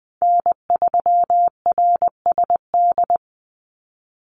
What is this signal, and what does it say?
Morse code, heard as a keyed tone, about 20 wpm, 700 Hz. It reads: N3RSD
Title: Call Signs Repeated 3x - 20wpm